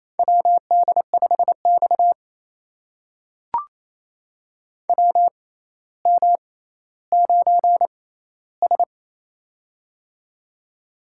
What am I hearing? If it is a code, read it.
Morse code, heard as a keyed tone, about 28 wpm, spaced as for 14 wpm, 700 Hz. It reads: WD5X WM9S
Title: USA Call Signs - 4 Characters - 6x Character Spacing - 28wpm